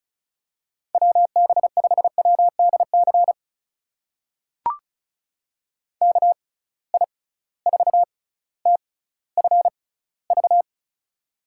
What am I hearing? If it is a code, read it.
Morse code, heard as a keyed tone, about 36 wpm, spaced as for 18 wpm, 700 Hz. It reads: WB5WDC KI4TFV
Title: USA Call Signs - 6 Characters - 6x Character Spacing - 35wpm